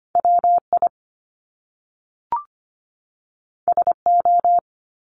Morse code, heard as a keyed tone, about 25 wpm, 700 Hz. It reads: WI SO
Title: Top Two Letter Combinations in Words 25wpm